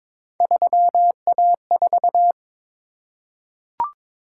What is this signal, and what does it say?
Morse code, heard as a keyed tone, about 22 wpm, 700 Hz. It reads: 3A4
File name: practice / 22wpm / Three Letter-Number Combinations - 22wpm